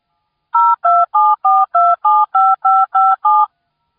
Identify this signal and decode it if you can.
DTMF, heard as telephone touch tones, tones of about 220 ms. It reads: 0274275557